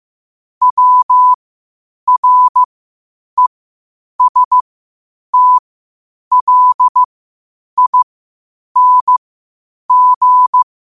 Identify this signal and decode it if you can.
Morse code, heard as a keyed tone, about 15 wpm, spaced as for 8 wpm, 1000 Hz. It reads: WRESTLING
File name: qotdfiles / 8wpm